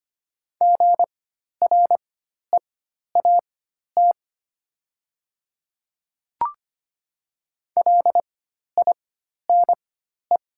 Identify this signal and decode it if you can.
Morse code, heard as a keyed tone, about 25 wpm, spaced as for 12 wpm, 700 Hz. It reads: GREAT LINE
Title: 4x Character Spacing - Top 200 Words - 25wpm